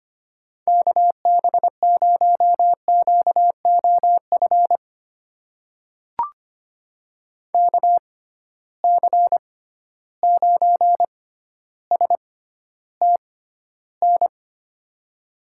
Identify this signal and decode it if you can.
Morse code, heard as a keyed tone, about 25 wpm, spaced as for 13 wpm, 700 Hz. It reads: KB0QOF KC9STN
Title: USA Call Signs - 6x Character Spacing - 25wpm